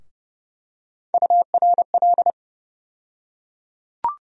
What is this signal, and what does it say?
Morse code, heard as a keyed tone, about 30 wpm, 700 Hz. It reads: URL
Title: USA Call Sign Postfixes - 3 Letters 30wpm